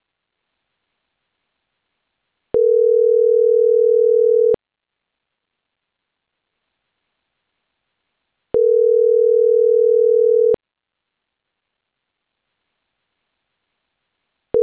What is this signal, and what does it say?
Ringback tone (North America), 440 + 480 Hz.